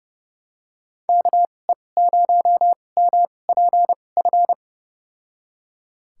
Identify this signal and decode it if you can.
Morse code, heard as a keyed tone, about 32 wpm, spaced as for 22 wpm, 700 Hz. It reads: KE0MPF
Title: USA Call Signs - 6 Characters - 2x Character Spacing - 30wpm